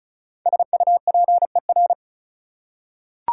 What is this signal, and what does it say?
Morse code, heard as a keyed tone, about 36 wpm, 700 Hz. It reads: SUPER
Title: Prefixes and Suffixes 35wpm